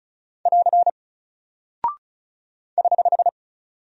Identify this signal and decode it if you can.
Morse code, heard as a keyed tone, about 36 wpm, 700 Hz. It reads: <AR> <HH>
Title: Single Character (Letter:Number:Punctuation:Prosign) Alt 35wpm